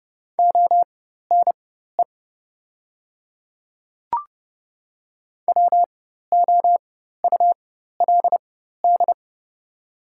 Morse code, heard as a keyed tone, about 30 wpm, spaced as for 14 wpm, 700 Hz. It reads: ONE WOULD
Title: 4x Character Spacing - Top 200 Words - 30wpm